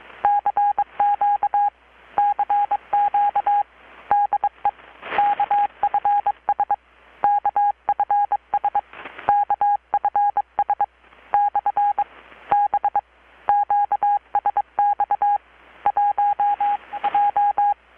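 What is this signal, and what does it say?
Morse code, heard as a keyed tone, about 23 wpm, 820 Hz. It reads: CQ CQ DE KFS KFS KFS / B QSX 1 J
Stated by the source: Morse from KPH's Night of Nights XX maritime radiotelegraphy event, received in Chiapas, Mexico about 3,800 km away on a Kenwood R-600 with a grounded 10 m long-wire antenna.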